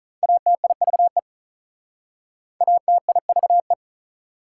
Morse code, heard as a keyed tone, about 46 wpm, 690 Hz. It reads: ATIVE ATIVE